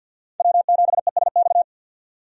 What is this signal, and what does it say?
Morse code, heard as a keyed tone, about 52 wpm, 700 Hz. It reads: W7ESX
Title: USA Call Signs - 5 Characters - 50wpm